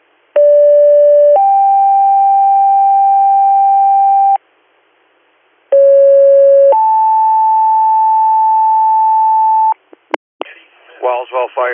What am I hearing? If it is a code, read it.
Two-tone page, heard as a one-second tone, then a three-second tone, about 580 then 790 Hz, about 550 then 880 Hz, with speech. It is heard from 0.4 to 9.7 s.